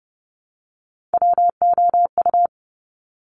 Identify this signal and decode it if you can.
Morse code, heard as a keyed tone, about 30 wpm, 700 Hz. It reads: WOU
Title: Top Three Letter Combinations in Words 30wpm